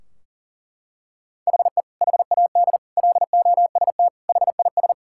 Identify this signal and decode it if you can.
Morse code, heard as a keyed tone, about 37 wpm, 700 Hz. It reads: HE HAD LOST HIS
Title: Sentences from Top 1600 Words 40wpm